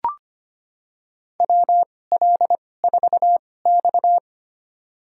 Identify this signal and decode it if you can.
Morse code, heard as a keyed tone, about 25 wpm, spaced as for 18 wpm, 700 Hz. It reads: WL4X
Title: USA Call Signs - 4 Characters - 2x Character Spacing - 25wpm